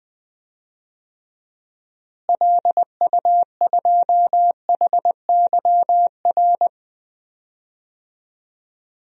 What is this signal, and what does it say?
Morse code, heard as a keyed tone, about 21 wpm, 700 Hz. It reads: LU2HYR
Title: DXCC-100 Call Signs - 6 Characters - Repeated 3x - 20wpm